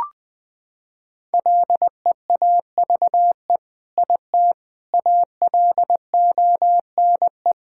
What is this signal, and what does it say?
Morse code, heard as a keyed tone, about 20 wpm, 700 Hz. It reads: LEAVE IT ALONE
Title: Sentences from Top 1000 Words - Review Part 5 - 20wpm